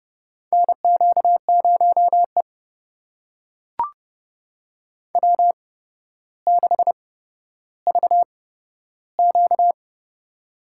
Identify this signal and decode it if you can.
Morse code, heard as a keyed tone, about 30 wpm, spaced as for 13 wpm, 700 Hz. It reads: NQ0E W6VQ
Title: USA Call Signs - 4 Characters - 8x Character Spacing - 30wpm